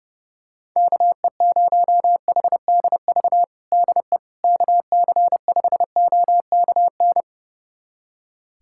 Morse code, heard as a keyed tone, about 30 wpm, 700 Hz. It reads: KE0HDV DE KC5OKN
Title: USA Callsign DE Callsign 30wpm